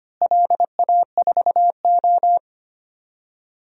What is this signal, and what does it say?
Morse code, heard as a keyed tone, about 25 wpm, 700 Hz. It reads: LA4O
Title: DXCC-100 Call Signs - 4 Characters - Repeated 3x - 25wpm